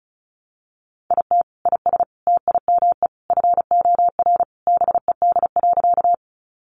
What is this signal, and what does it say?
Morse code, heard as a keyed tone, about 34 wpm, 700 Hz. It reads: IT IS TIME FOR BED.
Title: Sentences from Top 500 Words 35wpm